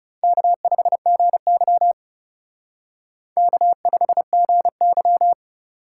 Morse code, heard as a keyed tone, about 36 wpm, 700 Hz. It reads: K5GY K5GY
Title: Speed Racing - USA 1x2 Call Signs 30wpm